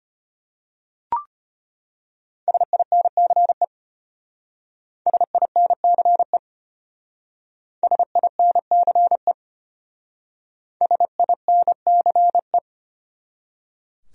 Morse code, heard as a keyed tone, about 31 wpm, 700 Hz. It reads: SINCE SINCE SINCE SINCE
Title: Speed-Racing Top 300 Words 25wpm